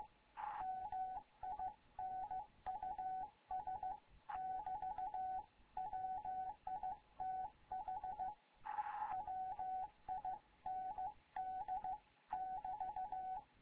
Morse code, heard as a keyed tone, about 15 wpm, 750 Hz. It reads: MINUS <BT> WITH WIND <BT>
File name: Morze-HA2MN / hangp / 2 11 GYAKORLAT